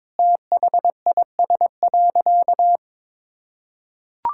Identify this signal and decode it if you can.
Morse code, heard as a keyed tone, about 22 wpm, 700 Hz. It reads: THIS.
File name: practice / 22wpm / Sentences with 4-Word Phrases - 22wpm